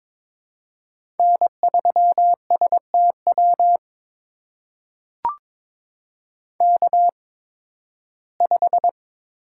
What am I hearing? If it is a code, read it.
Morse code, heard as a keyed tone, about 22 wpm, spaced as for 14 wpm, 700 Hz. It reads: N3STW K5
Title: USA Call Signs - 5 Characters - 8x Character Spacing - 22wpm